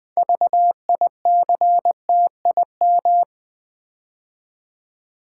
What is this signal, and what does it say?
Morse code, heard as a keyed tone, about 20 wpm, 700 Hz. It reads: VICTIM
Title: 6 Letter Words - First 3 Letters Given - 20wpm